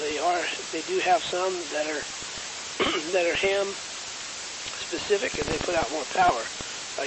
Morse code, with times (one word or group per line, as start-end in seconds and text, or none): none